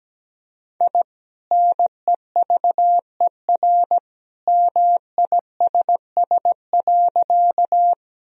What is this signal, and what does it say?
Morse code, heard as a keyed tone, about 17 wpm, 700 Hz. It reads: I NEVER MISS.
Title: Sentences from Top 1000 Words - Review Part 3 - 17wpm